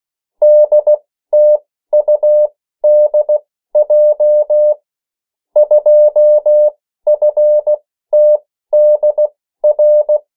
Morse code, heard as a keyed tone, about 16 wpm, spaced as for 13 wpm, 600 Hz. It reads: DTUDJ 2FTDR
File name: morse mp3 / 14 urok 13